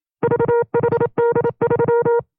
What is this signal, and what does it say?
Morse code, heard as a keyed tone, about 27 wpm, 470 Hz. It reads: VHD3
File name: FreeFiles / CW-practice / 25WPM-side1